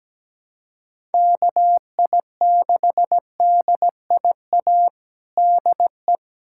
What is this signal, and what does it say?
Morse code, heard as a keyed tone, about 17 wpm, 700 Hz. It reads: KI6DIA DE
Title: USA Callsign DE Callsign 17wpm